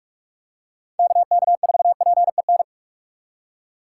Morse code, heard as a keyed tone, about 42 wpm, 700 Hz. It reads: KK4PEN
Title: USA Call Signs - 6 Characters - 4x Character Spacing - 45wpm